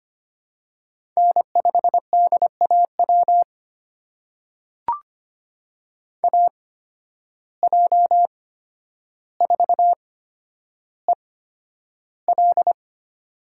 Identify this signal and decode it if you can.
Morse code, heard as a keyed tone, about 25 wpm, spaced as for 11 wpm, 700 Hz. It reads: N5DAW AJ4EL
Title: USA Call Signs - 5 Characters - 8x Character Spacing - 25wpm